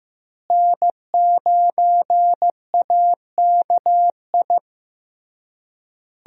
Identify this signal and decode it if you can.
Morse code, heard as a keyed tone, about 15 wpm, 700 Hz. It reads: N9AKI
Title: USA Call Signs - 5 Characters - Repeated 3x - 15wpm